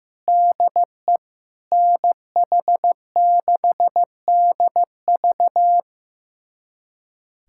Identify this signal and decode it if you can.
Morse code, heard as a keyed tone, about 16 wpm, 700 Hz. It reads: DE NH6DV
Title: USA Callsign DE Callsign - 5 Characters - 15wpm